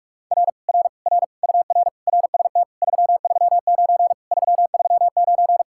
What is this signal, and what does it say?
Morse code, heard as a keyed tone, about 40 wpm, 700 Hz. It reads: R R R UR RST 339 339